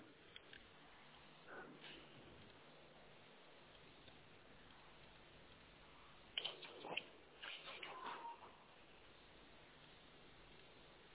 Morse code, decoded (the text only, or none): none